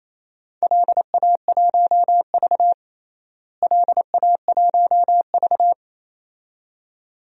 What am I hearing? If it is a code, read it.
Morse code, heard as a keyed tone, about 28 wpm, 700 Hz. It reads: LA1V LA1V
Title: DXCC-100 Call Signs - 4 Characters - Repeated 2x - 28wpm